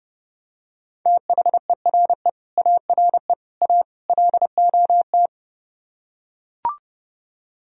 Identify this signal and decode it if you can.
Morse code, heard as a keyed tone, about 32 wpm, 700 Hz. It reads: THERE ARE A LOT